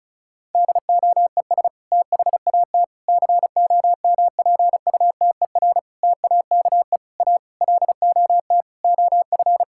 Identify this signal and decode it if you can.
Morse code, heard as a keyed tone, about 36 wpm, 700 Hz. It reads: DOES THAT COMPUTER TAKE A LOT OF
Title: Sentences from Top 1000 Words - Review Part 6 - 35wpm